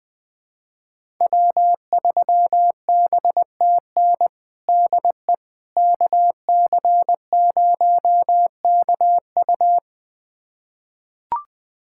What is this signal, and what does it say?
Morse code, heard as a keyed tone, about 20 wpm, 700 Hz. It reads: W3BTN DE KC0KU
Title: USA Callsign DE Callsign - 5 Characters - 20wpm